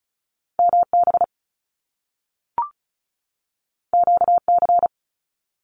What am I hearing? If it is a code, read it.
Morse code, heard as a keyed tone, about 36 wpm, 700 Hz. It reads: MB QC
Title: Canadian Provinces and Territories 35wpm